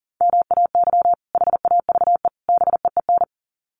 Morse code, heard as a keyed tone, about 40 wpm, 700 Hz. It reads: MAY HAVE BEEN
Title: Sets of 3 Words 40wpm